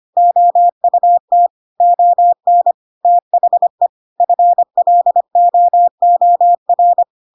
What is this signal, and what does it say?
Morse code, heard as a keyed tone, about 25 wpm, 700 Hz. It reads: OUT ON THE FLOOR
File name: Pinocchio / Pinocchio ch7 25 25 ik1hge